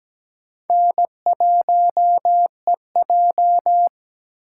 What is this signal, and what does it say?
Morse code, heard as a keyed tone, about 20 wpm, 700 Hz. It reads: N1EJ
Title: Speed Racing - USA 1x2 Call Signs 17wpm